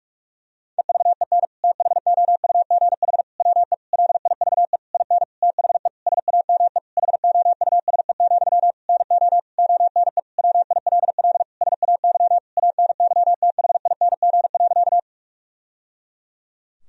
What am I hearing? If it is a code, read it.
Morse code, heard as a keyed tone, about 46 wpm, 700 Hz. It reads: EVEN THOUGH WE LIVE IN THE SAME HOUSE, NO ONE WILL SAY ANYTHING.
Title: Sentences from Top 1000 Words - Review Part 2 - 45wpm